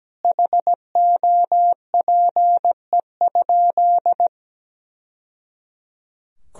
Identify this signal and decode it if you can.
Morse code, heard as a keyed tone, about 17 wpm, 700 Hz. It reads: HOPE?
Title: Sentences from Top 1000 Words - Review Part 3 - 17wpm